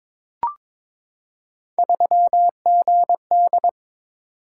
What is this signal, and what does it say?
Morse code, heard as a keyed tone, about 22 wpm, 700 Hz. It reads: 3GD
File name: practice / 22wpm / Three Letter-Number Combinations - 22wpm